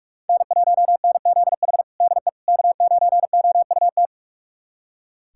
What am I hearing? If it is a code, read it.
Morse code, heard as a keyed tone, about 46 wpm, 700 Hz. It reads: N1NZH DE K9OUT
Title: USA Callsign DE Callsign - 5 Characters - 45wpm